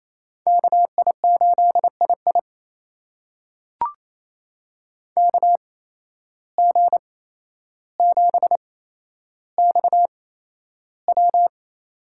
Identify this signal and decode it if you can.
Morse code, heard as a keyed tone, about 28 wpm, spaced as for 12 wpm, 700 Hz. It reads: KI8II KG7XW
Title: USA Call Signs - 5 Characters - 8x Character Spacing - 28wpm